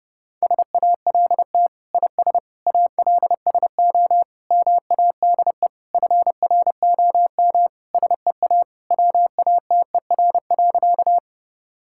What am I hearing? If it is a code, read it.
Morse code, heard as a keyed tone, about 30 wpm, 700 Hz. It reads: SALT IS ALSO MADE FROM SEA WATER.
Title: Sentences from Top 1000 Words - Review Part 5 - 30wpm